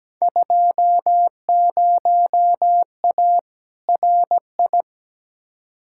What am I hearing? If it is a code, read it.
Morse code, heard as a keyed tone, about 17 wpm, 700 Hz. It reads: 20A RI
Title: Field Day Exchanges Encore - 17wpm